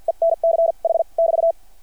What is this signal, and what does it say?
Morse code, heard as a keyed tone, about 53 wpm, spaced as for 42 wpm, 660 Hz. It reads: ENCH <BT>
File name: archived / CRSnet